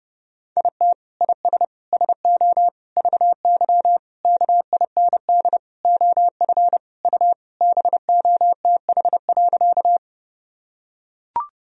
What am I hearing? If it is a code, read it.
Morse code, heard as a keyed tone, about 30 wpm, 700 Hz. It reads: IT IS SO VY KIND OF U BOTH.
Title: Sentences from Top 300 Words and Abbreviations 30wpm